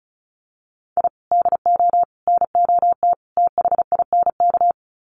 Morse code, heard as a keyed tone, about 34 wpm, 700 Hz. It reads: I DO NOT THINK
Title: Sets of 4 Words 35wpm